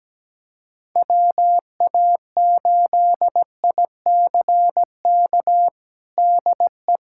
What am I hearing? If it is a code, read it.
Morse code, heard as a keyed tone, about 17 wpm, 700 Hz. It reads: WA8ICK DE
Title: USA Callsign DE Callsign 17wpm